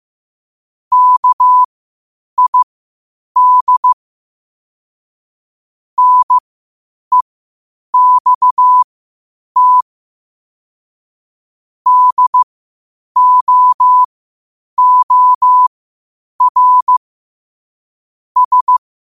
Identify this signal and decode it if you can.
Morse code, heard as a keyed tone, about 15 wpm, spaced as for 8 wpm, 1000 Hz. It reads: KID NEXT DOOR S
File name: qotdfiles / 8wpm